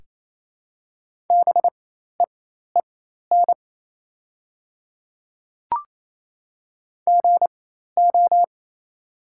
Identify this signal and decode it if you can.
Morse code, heard as a keyed tone, about 28 wpm, spaced as for 13 wpm, 700 Hz. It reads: BEEN GO
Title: 4x Character Spacing - Top 200 Words - 28wpm